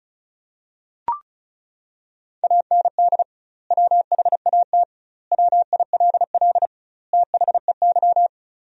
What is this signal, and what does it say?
Morse code, heard as a keyed tone, about 34 wpm, 700 Hz. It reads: AND WHAT WILL THEY
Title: 2x Word-Spacing - Sentences from Top 200 Words - 35wpm